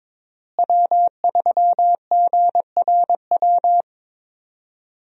Morse code, heard as a keyed tone, about 22 wpm, 700 Hz. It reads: W3GRW
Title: USA Call Signs - 5 Characters - Repeated 2x - 22wpm